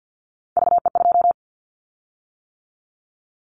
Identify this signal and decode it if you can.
Morse code, heard as a keyed tone, about 50 wpm, 700 Hz. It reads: VE2
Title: Callsign Prefixes for Top 50 DXCC - 50wpm